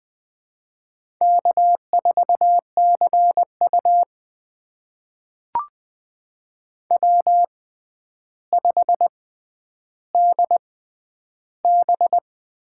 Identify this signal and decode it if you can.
Morse code, heard as a keyed tone, about 20 wpm, spaced as for 10 wpm, 700 Hz. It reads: K4CU W5DB
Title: USA Call Signs - 6x Character Spacing - 20wpm